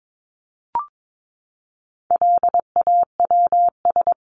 Morse code, heard as a keyed tone, about 22 wpm, 700 Hz. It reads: LAWS
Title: ES:S Suffix - Top 100 Words with Base Words 22wpm